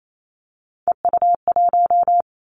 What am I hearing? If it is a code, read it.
Morse code, heard as a keyed tone, about 28 wpm, 700 Hz. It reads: EU1
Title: Callsign Prefixes for Top 50 DXCC - 28wpm